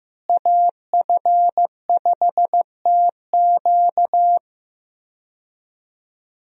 Morse code, heard as a keyed tone, about 15 wpm, 700 Hz. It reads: AF5TQ
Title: USA Call Signs - 5 Characters - Repeated 2x - 15wpm